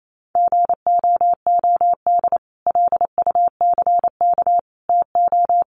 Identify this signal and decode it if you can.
Morse code, heard as a keyed tone, about 28 wpm, 700 Hz. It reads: GOOD LUCK TO
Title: Sentences from Top 1500 Words 28wpm